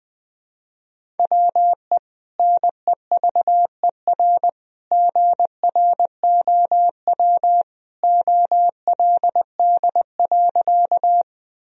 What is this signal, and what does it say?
Morse code, heard as a keyed tone, about 21 wpm, 700 Hz. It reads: WE NEVER GROW OLD.
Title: Sentences from Top 1000 Words - Review Part 1 - 20wpm